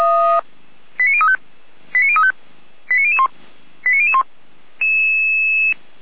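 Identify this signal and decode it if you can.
Five-tone selective call (ZVEI), heard as quick runs of five tones, tones of about 70 ms. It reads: 89035 89035 89001 89001
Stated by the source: ZVEI five-tone selective call 89340 received on an SDR.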